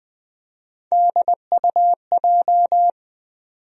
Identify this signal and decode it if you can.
Morse code, heard as a keyed tone, about 20 wpm, 700 Hz. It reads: DUJ
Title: USA Call Sign Postfixes - 3 Letters 20wpm